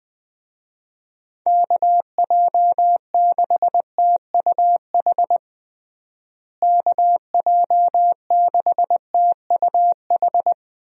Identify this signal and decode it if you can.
Morse code, heard as a keyed tone, about 21 wpm, 700 Hz. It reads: KJ6TUH KJ6TUH
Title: USA Call Signs - 6 Characters - Repeated 3x - 20wpm